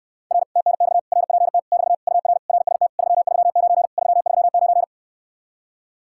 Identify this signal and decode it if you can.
Morse code, heard as a keyed tone, about 69 wpm, 700 Hz. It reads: R TNX RPT <BT> UR RST 339 339